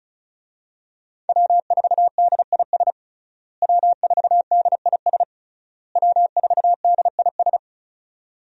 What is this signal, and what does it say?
Morse code, heard as a keyed tone, about 34 wpm, 700 Hz. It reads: W4DIS W4DIS W4DIS
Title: USA Call Signs - 5 Characters - Repeated 3x - 35wpm